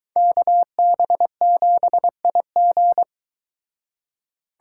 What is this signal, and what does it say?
Morse code, heard as a keyed tone, about 23 wpm, 700 Hz. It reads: KB7IG